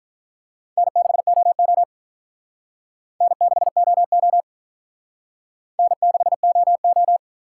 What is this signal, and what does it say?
Morse code, heard as a keyed tone, about 47 wpm, 700 Hz. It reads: N6OO N6OO N6OO